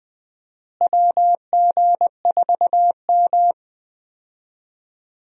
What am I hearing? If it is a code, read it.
Morse code, heard as a keyed tone, about 20 wpm, 700 Hz. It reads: WG4M
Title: USA Call Signs - 4 Characters - Repeated 3x - 20wpm